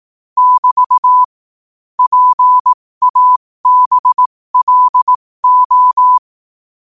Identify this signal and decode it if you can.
Morse code, heard as a keyed tone, about 18 wpm, spaced as for 15 wpm, 1000 Hz. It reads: <BT> PABLO
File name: qotdfiles / 15wpm